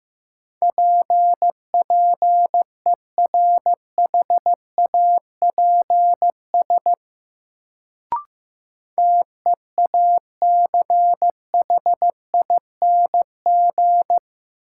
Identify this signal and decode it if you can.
Morse code, heard as a keyed tone, about 15 wpm, 700 Hz. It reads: PPERHAPS TEACHING
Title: Sentences from Top 1100 Words - New Word Drill Alt - 15wpm